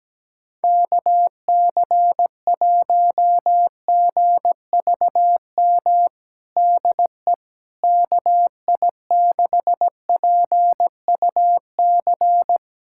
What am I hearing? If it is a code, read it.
Morse code, heard as a keyed tone, about 17 wpm, 700 Hz. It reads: KC1GVM DE KI6PUC
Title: USA Callsign DE Callsign 17wpm